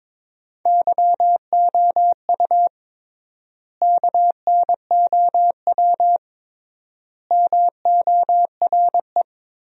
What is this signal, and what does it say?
Morse code, heard as a keyed tone, about 23 wpm, 700 Hz. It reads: YOU KNOW MORE
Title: 3x Word-Spacing - Sentences from Top 100 Words - 22wpm